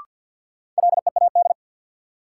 Morse code, heard as a keyed tone, about 51 wpm, 700 Hz. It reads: LEAD
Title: Sentences from Top 500 Words - New Word Drill - 50wpm